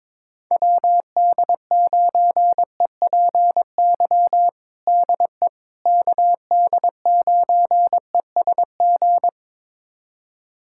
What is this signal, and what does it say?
Morse code, heard as a keyed tone, about 22 wpm, 700 Hz. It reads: WD9EPY DE KD9ESG
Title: USA Callsign DE Callsign 22wpm